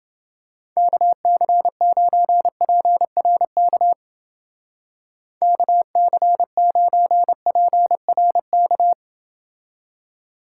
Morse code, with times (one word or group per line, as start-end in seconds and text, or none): 0.8-3.9 KC9PRK
5.4-8.9 KC9PRK